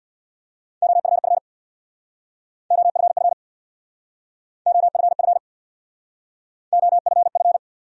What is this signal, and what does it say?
Morse code, heard as a keyed tone, about 70 wpm, 700 Hz. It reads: OFF OFF OFF OFF